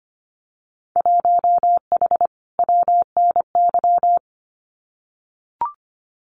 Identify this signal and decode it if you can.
Morse code, heard as a keyed tone, about 25 wpm, 700 Hz. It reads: 1H WNY
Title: Winter Field Day Exchanges 25wpm